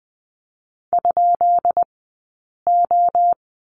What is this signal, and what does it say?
Morse code, heard as a keyed tone, about 20 wpm, spaced as for 8 wpm, 700 Hz. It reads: ?O
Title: 204 Lesson 33d - Letter j - Two Characters 20wpm